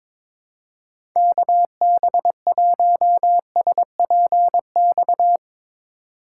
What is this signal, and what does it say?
Morse code, heard as a keyed tone, about 22 wpm, 700 Hz. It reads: KB1SPX
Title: USA Call Signs - 6 Characters - 3x Character Spacing - 22wpm